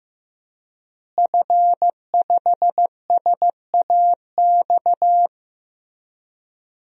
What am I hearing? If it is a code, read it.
Morse code, heard as a keyed tone, about 16 wpm, 700 Hz. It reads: F5SAX
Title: DXCC-100 Call Signs - 5 Characters - Repeated 3x - 15wpm